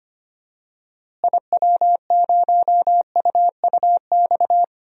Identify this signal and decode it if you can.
Morse code, heard as a keyed tone, about 25 wpm, 700 Hz. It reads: IW0UUX
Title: DXCC-100 Call Signs - 6 Characters - Repeated 2x - 25wpm